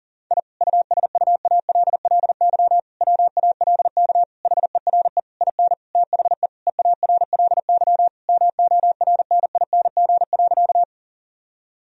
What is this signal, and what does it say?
Morse code, heard as a keyed tone, about 40 wpm, 700 Hz. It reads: I USUALLY WALK HERE IN THE EARLY MORNING.
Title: Sentences from Top 1000 Words - Review Part 3 - 40wpm